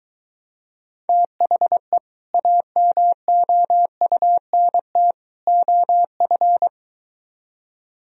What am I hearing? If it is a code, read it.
Morse code, heard as a keyed tone, about 23 wpm, 700 Hz. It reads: THE AMOUNT OF